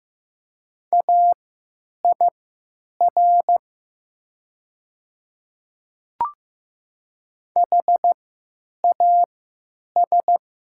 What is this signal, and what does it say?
Morse code, heard as a keyed tone, about 15 wpm, spaced as for 9 wpm, 700 Hz. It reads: AIR HAS
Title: 3x Character Spacing - Top 300 Words - 15wpm